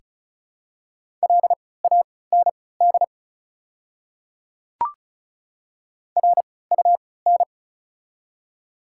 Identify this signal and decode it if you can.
Morse code, heard as a keyed tone, about 36 wpm, spaced as for 20 wpm, 700 Hz. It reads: LAND RUN
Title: 3x Character Spacing - Top 500 Words - 35wpm